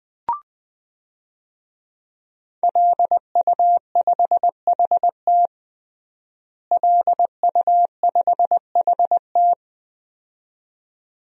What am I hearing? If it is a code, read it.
Morse code, heard as a keyed tone, about 20 wpm, 700 Hz. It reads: LU5HT LU5HT
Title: DXCC-100 Call Signs - 5 Characters - Repeated 2x - 20wpm